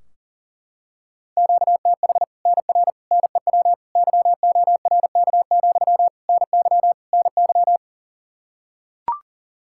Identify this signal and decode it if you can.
Morse code, heard as a keyed tone, about 40 wpm, 700 Hz. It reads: QTH NR NEW YORK, NY NY
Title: Practice QSO - Exchange 1 - Beginning - 40wpm